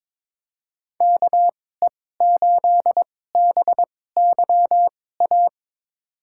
Morse code, heard as a keyed tone, about 22 wpm, spaced as for 16 wpm, 700 Hz. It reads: KE8BYA
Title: USA Call Signs - 6 Characters - 2x Character Spacing - 22wpm